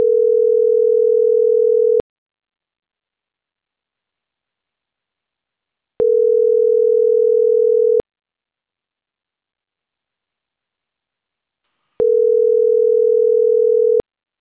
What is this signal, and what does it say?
Ringback tone (North America), 440 + 480 Hz.